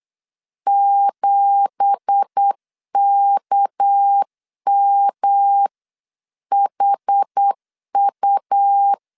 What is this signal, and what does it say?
Morse code, heard as a keyed tone, about 8 wpm, 800 Hz. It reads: 7KMHU
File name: CW / DET / 1 9 1500C S C 8wpm 5653